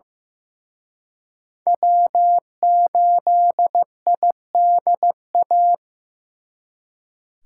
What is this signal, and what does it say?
Morse code, heard as a keyed tone, about 18 wpm, 700 Hz. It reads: W8IDA
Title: Speed Racing - USA 1x3 Call Signs 15wpm